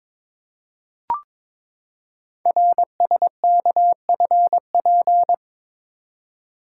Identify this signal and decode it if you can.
Morse code, heard as a keyed tone, about 23 wpm, 700 Hz. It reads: RSKFP
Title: Code Groups - 5 Letters - 22wpm